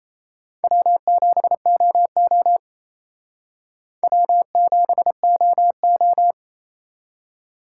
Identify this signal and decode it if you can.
Morse code, heard as a keyed tone, about 34 wpm, 700 Hz. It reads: W7OO W7OO
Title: Speed Racing - USA 1x2 Call Signs 28wpm